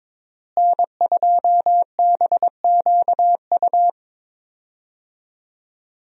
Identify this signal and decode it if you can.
Morse code, heard as a keyed tone, about 22 wpm, 700 Hz. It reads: N2BQU
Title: USA Call Signs - 5 Characters - Repeated 3x - 22wpm